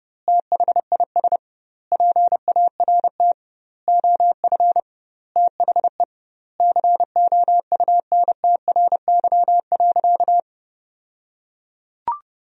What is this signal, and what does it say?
Morse code, heard as a keyed tone, about 30 wpm, 700 Hz. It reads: THIS PART OF THE COUNTRY.
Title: 2x Word-Spacing - Sentences from Top 300 Words - 30wpm